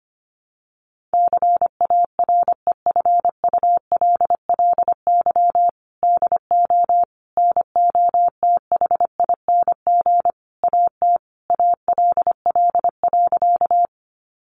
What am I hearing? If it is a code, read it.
Morse code, heard as a keyed tone, about 25 wpm, 700 Hz. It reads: CAREFULLY DO NOTHING AT ALL.
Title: Sentences from Top 1300 Words 25wpm